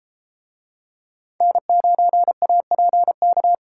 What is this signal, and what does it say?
Morse code, heard as a keyed tone, about 30 wpm, 700 Hz. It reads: N9APK